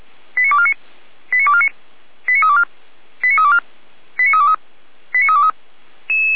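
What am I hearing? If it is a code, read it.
Five-tone selective call (ZVEI), heard as quick runs of five tones, tones of about 70 ms. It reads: 89249 89249 89324 89324 89323 89323
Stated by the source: ZVEI five-tone call received by SDR, from the feuerwehr-orlamuende.org fire brigade audio archive.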